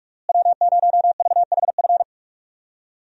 Morse code, heard as a keyed tone, about 46 wpm, 700 Hz. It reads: W0VHF
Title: USA Call Signs - 5 Characters - Repeated 3x - 45wpm